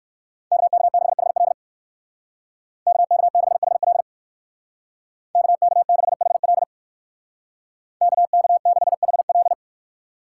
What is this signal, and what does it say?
Morse code, heard as a keyed tone, about 59 wpm, 700 Hz. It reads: KK6HL KK6HL KK6HL KK6HL